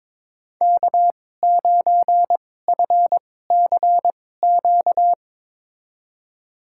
Morse code, heard as a keyed tone, about 23 wpm, spaced as for 16 wpm, 700 Hz. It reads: K9FCQ
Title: USA Call Signs - 5 Characters - 2x Character Spacing - 22wpm